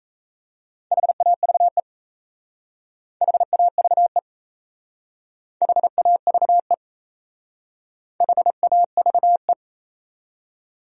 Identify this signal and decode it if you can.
Morse code, heard as a keyed tone, about 35 wpm, 700 Hz. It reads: HAVE HAVE HAVE HAVE